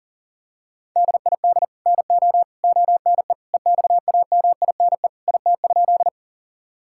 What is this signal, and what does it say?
Morse code, heard as a keyed tone, about 40 wpm, 700 Hz. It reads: DID NO ONE EXAMINE IT?
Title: Sentences from Top 1600 Words 40wpm